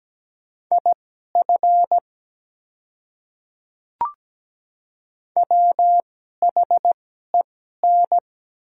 Morse code, heard as a keyed tone, about 17 wpm, spaced as for 12 wpm, 700 Hz. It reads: IF WHEN
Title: 2x Character Spacing - Top 100 Words - 17wpm